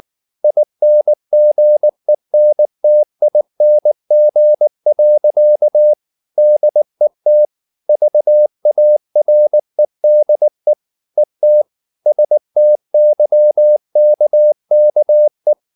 Morse code, heard as a keyed tone, about 19 wpm, 600 Hz. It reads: INGENTING. DET VAREDE ET STYKKE
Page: Ferie 19 wpm